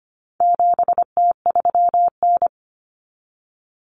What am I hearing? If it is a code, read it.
Morse code, heard as a keyed tone, about 25 wpm, 700 Hz. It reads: 7T3N
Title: Common Cut Numbers - 4-Digit 25wpm